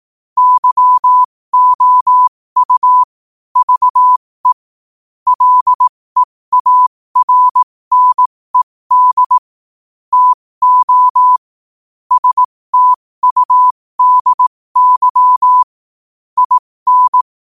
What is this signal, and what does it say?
Morse code, heard as a keyed tone, about 18 wpm, spaced as for 15 wpm, 1000 Hz. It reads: YOU VE LEARNED TO STUDY IN
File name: qotdfiles / 15wpm